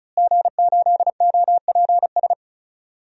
Morse code, heard as a keyed tone, about 36 wpm, 700 Hz. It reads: G8OPS
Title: DXCC-100 Call Signs - 5 Characters - Repeated 2x - 35wpm